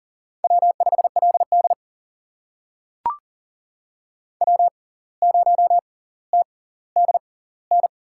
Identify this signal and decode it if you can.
Morse code, heard as a keyed tone, about 40 wpm, spaced as for 19 wpm, 700 Hz. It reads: W5LD W0TDN
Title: USA Call Signs - 6x Character Spacing - 40wpm